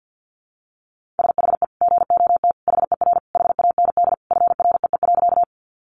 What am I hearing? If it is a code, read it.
Morse code, heard as a keyed tone, about 50 wpm, 700 Hz. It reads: SHE GOT HER HAND FREE.
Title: Sentences from Top 500 Words 50wpm